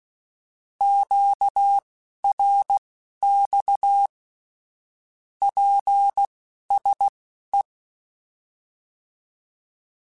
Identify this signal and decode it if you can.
Morse code, heard as a keyed tone, about 16 wpm, spaced as for 11 wpm, 800 Hz. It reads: QRX PSE